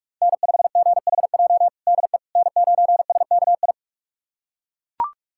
Morse code, heard as a keyed tone, about 41 wpm, 700 Hz. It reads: N5GHJ DE N9SKI
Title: USA Callsign DE Callsign - 5 Characters - 45wpm